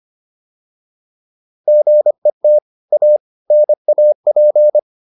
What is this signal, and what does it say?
Morse code, heard as a keyed tone, about 25 wpm, 600 Hz. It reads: GET A NAP